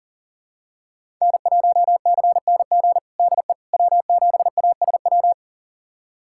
Morse code, heard as a keyed tone, about 40 wpm, 700 Hz. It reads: N1CNG DE W7ASW
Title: USA Callsign DE Callsign - 5 Characters - 40wpm